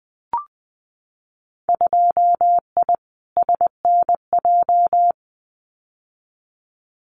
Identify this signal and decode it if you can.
Morse code, heard as a keyed tone, about 20 wpm, 700 Hz. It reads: 2I SNJ
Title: Winter Field Day Exchanges Encore 20wpm